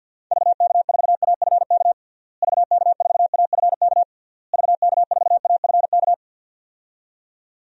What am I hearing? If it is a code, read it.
Morse code, heard as a keyed tone, about 51 wpm, 700 Hz. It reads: VK4AFK VK4AFK VK4AFK
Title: DXCC-100 Call Signs - 6 Characters - Repeated 3x - 50wpm